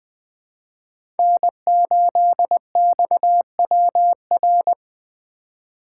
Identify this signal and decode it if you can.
Morse code, heard as a keyed tone, about 20 wpm, 700 Hz. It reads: N8XWR
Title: USA Call Signs - 4x Character Spacing - 20wpm